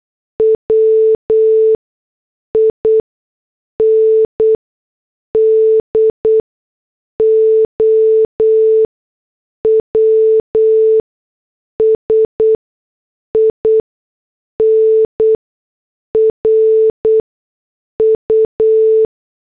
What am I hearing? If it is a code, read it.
Morse code, heard as a keyed tone, about 8 wpm, spaced as for 6 wpm, 440 Hz. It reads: WINDOWS IN RU
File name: morse / headlines